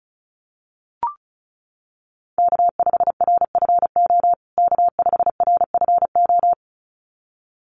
Morse code, heard as a keyed tone, about 34 wpm, 700 Hz. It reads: K5RFO K5RFO
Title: USA Call Signs Repeated 2x 35wpm